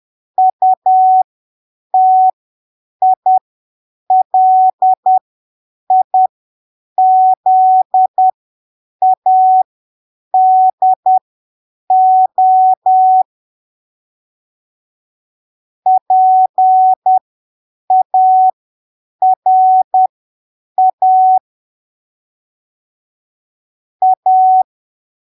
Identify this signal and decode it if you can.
Morse code, heard as a keyed tone, about 10 wpm, spaced as for 7 wpm, 750 Hz. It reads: UTILIZADO PARA A
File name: arquivos / texto / texto 10 10ppm